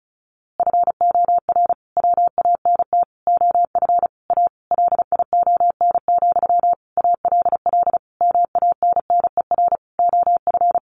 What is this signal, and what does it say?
Morse code, heard as a keyed tone, about 34 wpm, 700 Hz. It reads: FOR WANT OF A LION, ALL MANNER OF
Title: A Sound Of Thunder 35wpm